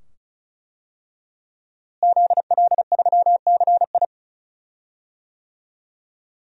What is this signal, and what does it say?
Morse code, heard as a keyed tone, about 36 wpm, 700 Hz. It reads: ZL3CI
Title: DXCC-100 Call Signs - 5 Characters - Repeated 2x - 35wpm